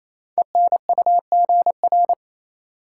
Morse code, heard as a keyed tone, about 27 wpm, 700 Hz. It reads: ENUGR
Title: Code Groups - 5 Letters - 28wpm